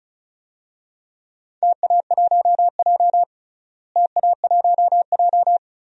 Morse code, heard as a keyed tone, about 36 wpm, 700 Hz. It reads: TA1J TA1J
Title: DXCC-100 Call Signs - 4 Characters - Repeated 3x - 35wpm